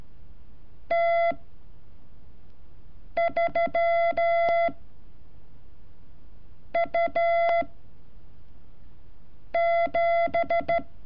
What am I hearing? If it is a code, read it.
Morse code, heard as a keyed tone, about 13 wpm, 680 Hz. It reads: T 3 U 7
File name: curso / ejercicio 31